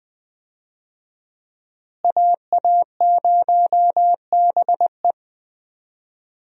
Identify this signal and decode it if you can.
Morse code, heard as a keyed tone, about 20 wpm, 700 Hz. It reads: AA0BE
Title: USA Call Signs - 5 Characters - Repeated 3x - 20wpm